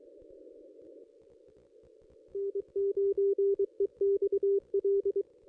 Morse code, heard as a keyed tone, about 23 wpm, 400 Hz. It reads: N9EXL
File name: contests / ss-cw-k5tr-op-n5rz -1400